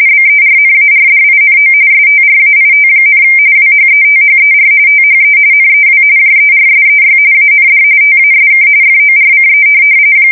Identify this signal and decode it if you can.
RTTY, about 45.45 baud. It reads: BERNARDINO STRAIT, WHETHER THE NEW JERSEY AND TASK FORCE
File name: fsk170 / USS-New-Jersey